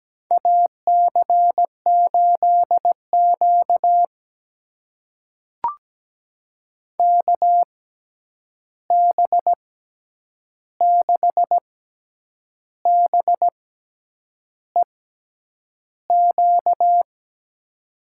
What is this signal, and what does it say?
Morse code, heard as a keyed tone, about 17 wpm, spaced as for 8 wpm, 700 Hz. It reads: AC8Q KB6BEQ
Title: USA Call Signs - 6x Character Spacing - 17wpm